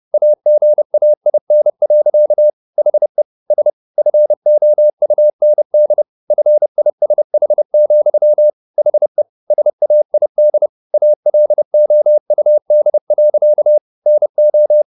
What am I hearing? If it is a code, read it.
Morse code, heard as a keyed tone, about 30 wpm, 600 Hz. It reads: AGAIN. HE S FOUND FISH, HE SAID ALOUD. NO
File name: the old man and the sea / the old man and the sea 1 30wpm